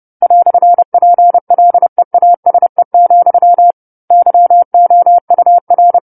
Morse code, heard as a keyed tone, about 30 wpm, 700 Hz. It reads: "PLEASE, YOUR